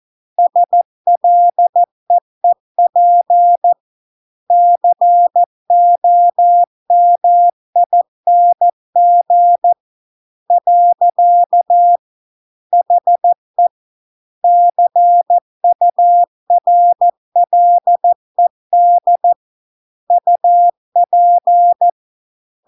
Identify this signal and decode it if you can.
Morse code, heard as a keyed tone, about 14 wpm, 700 Hz. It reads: SLEEP COMING . HE CURLED UP